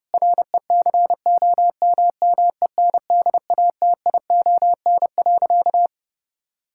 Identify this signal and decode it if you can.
Morse code, heard as a keyed tone, about 32 wpm, 700 Hz. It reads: RECOMMENDATION.
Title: Sentences from Top 1600 Words 30wpm